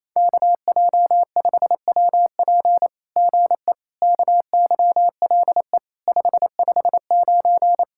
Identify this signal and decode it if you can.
Morse code, heard as a keyed tone, about 28 wpm, 700 Hz. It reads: KJ5WP GE KYLE 559
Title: SOTA QSO Practice 28wpm